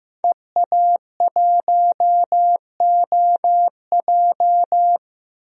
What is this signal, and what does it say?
Morse code, heard as a keyed tone, about 15 wpm, 700 Hz. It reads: EA1OJ
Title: DXCC-100 Call Signs - 5 Characters - 15wpm